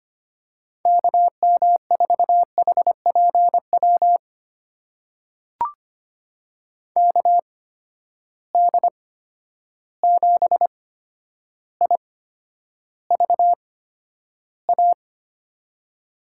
Morse code, heard as a keyed tone, about 25 wpm, spaced as for 11 wpm, 700 Hz. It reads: KM4HPW KD7IVA
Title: USA Call Signs - 6 Characters - 8x Character Spacing - 25wpm